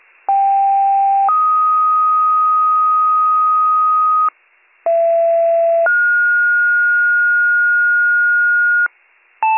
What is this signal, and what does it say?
Two-tone page, heard as a one-second tone, then a three-second tone, about 790 then 1250 Hz, about 670 then 1430 Hz, with noise. It is heard from 0.3 to 8.9 s.